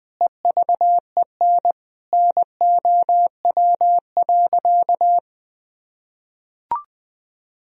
Morse code, heard as a keyed tone, about 20 wpm, 700 Hz. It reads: EVEN NOW.
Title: Sentences from Top 1000 Words - Review Part 1 - 20wpm